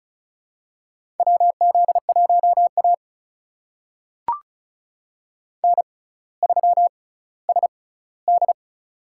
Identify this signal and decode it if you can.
Morse code, heard as a keyed tone, about 36 wpm, spaced as for 18 wpm, 700 Hz. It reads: WZ1A N3SD
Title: USA Call Signs - 4 Characters - 6x Character Spacing - 35wpm